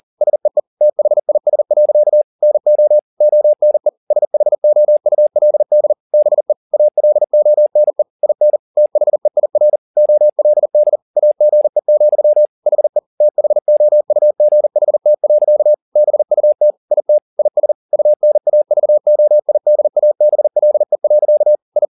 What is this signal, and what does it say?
Morse code, heard as a keyed tone, about 40 wpm, 600 Hz. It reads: SEE THIS. NO ONE SHOULD BE ALONE IN THEIR OLD AGE, HE THOUGHT. BUT IT IS UNAVOIDABLE. I